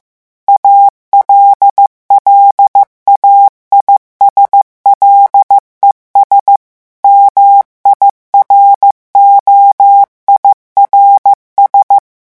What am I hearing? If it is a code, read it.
Morse code, heard as a keyed tone, about 15 wpm, 800 Hz. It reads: ALLAIS LES MIROIRS
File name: morse mp3 / revision9 citations humoristiques vitesse 14 mots minute